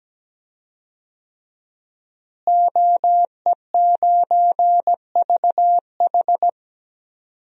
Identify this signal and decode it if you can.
Morse code, heard as a keyed tone, about 17 wpm, 700 Hz. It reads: OE9VH
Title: DXCC-100 Call Signs - 5 Characters - Repeated 3x - 17wpm